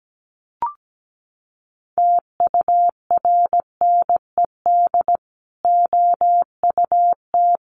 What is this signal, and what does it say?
Morse code, heard as a keyed tone, about 17 wpm, 700 Hz. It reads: TURNED OUT
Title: Sets of 4 Words 17wpm